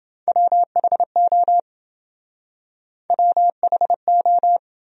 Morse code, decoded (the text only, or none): WHO WHO